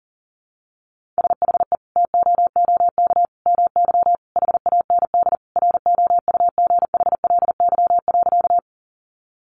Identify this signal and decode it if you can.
Morse code, heard as a keyed tone, about 37 wpm, 700 Hz. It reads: SHE TOOK MY HAND ROUGHLY.
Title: Sentences from Top 1500 Words 40wpm